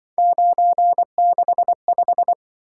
Morse code, heard as a keyed tone, about 24 wpm, 700 Hz. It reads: 965
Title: Speed-Racing - Three Letter-Number Combinations - 20wpm